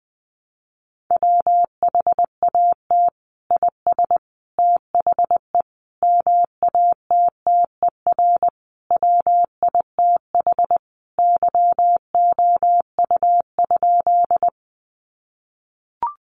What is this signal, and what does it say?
Morse code, heard as a keyed tone, about 20 wpm, 700 Hz. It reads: WHAT IS THE MATTER WITH YOU?
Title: Sentences from Top 700 Words 20wpm